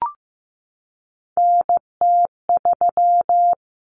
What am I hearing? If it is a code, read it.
Morse code, heard as a keyed tone, about 15 wpm, 700 Hz. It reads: NT3
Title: Common Cut Numbers - 3-Digit 15wpm